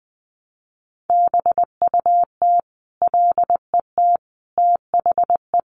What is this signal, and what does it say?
Morse code, heard as a keyed tone, about 19 wpm, 700 Hz. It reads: BUT LET THE
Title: Sentences from Top 1300 Words 20wpm